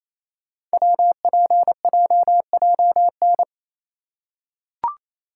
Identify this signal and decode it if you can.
Morse code, heard as a keyed tone, about 27 wpm, 700 Hz. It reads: WPJJN
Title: Code Groups - 5 Letters - 28wpm